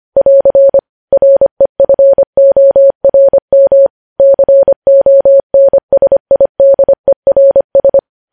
Morse code, heard as a keyed tone, about 25 wpm, 550 Hz. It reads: <AR> REFORM CONSIDERS
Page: News Headlines in Morse Code at 25 WPM